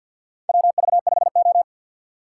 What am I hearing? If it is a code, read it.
Morse code, heard as a keyed tone, about 53 wpm, 700 Hz. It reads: WV5O
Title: USA Call Signs - 3x Character Spacing - 50wpm